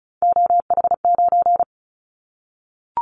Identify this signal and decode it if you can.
Morse code, heard as a keyed tone, about 34 wpm, 700 Hz. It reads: OH9
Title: Callsign Prefixes for Top 25 DXCC - Repeated 3x - 35wpm